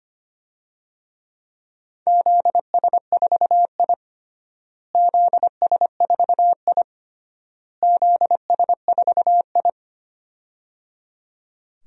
Morse code, heard as a keyed tone, about 25 wpm, 700 Hz. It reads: ZS4I ZS4I ZS4I
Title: DXCC-100 Call Signs - 4 Characters - Repeated 3x - 25wpm